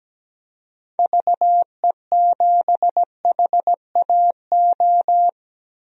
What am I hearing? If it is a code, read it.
Morse code, heard as a keyed tone, about 17 wpm, 700 Hz. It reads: VE7HAO
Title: DXCC-100 Call Signs - 6 Characters - 17wpm